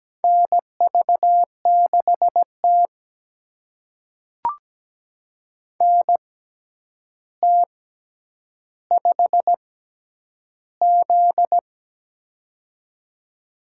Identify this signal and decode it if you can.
Morse code, heard as a keyed tone, about 17 wpm, spaced as for 9 wpm, 700 Hz. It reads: NV6T NT5Z
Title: USA Call Signs - 4 Characters - 6x Character Spacing - 17wpm